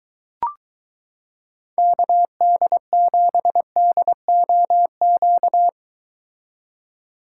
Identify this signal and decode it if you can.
Morse code, heard as a keyed tone, about 23 wpm, 700 Hz. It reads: KD7DOQ